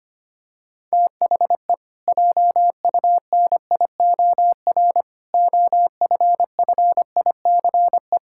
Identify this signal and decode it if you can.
Morse code, heard as a keyed tone, about 25 wpm, 700 Hz. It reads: THE JUNIOR OFFICE
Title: Pangram Challenge v2 - 25wpm